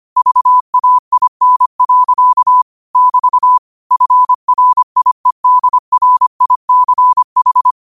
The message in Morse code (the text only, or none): VAIN. <BT> FRIEDRICH